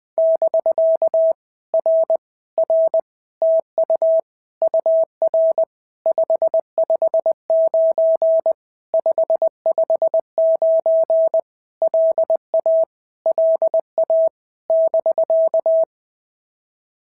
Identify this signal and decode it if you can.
Morse code, heard as a keyed tone, about 20 wpm, 650 Hz. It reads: <BK> R R TU UR 559 559 LA LA <BK>
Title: POTA QSO Practice - 20wpm